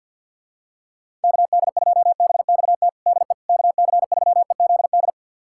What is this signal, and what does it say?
Morse code, heard as a keyed tone, about 52 wpm, 700 Hz. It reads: KD2BXT DE KC3EZD
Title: USA Callsign DE Callsign 50wpm